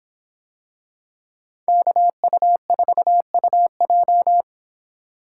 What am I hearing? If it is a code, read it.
Morse code, heard as a keyed tone, about 25 wpm, 700 Hz. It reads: KU4UJ